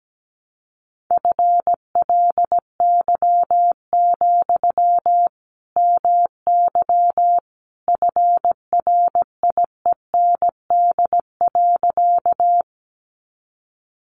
Morse code, heard as a keyed tone, about 17 wpm, 700 Hz. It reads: FLY, MY FRIEND.
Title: Sentences from Top 500 Words 17wpm